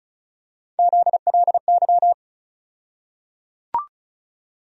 Morse code, heard as a keyed tone, about 36 wpm, 700 Hz. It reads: ZLY
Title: USA Call Sign Postfixes - 3 Letters 35wpm